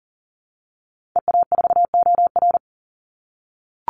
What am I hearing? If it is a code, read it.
Morse code, heard as a keyed tone, about 40 wpm, 700 Hz. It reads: EA4OR
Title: Call Signs - 40wpm